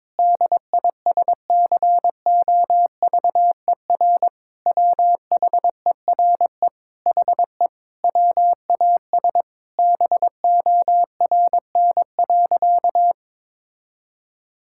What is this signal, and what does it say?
Morse code, heard as a keyed tone, about 22 wpm, 700 Hz. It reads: DISCOVER WHERE HE WAS BORN.
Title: Sentences from Top 1600 Words 22wpm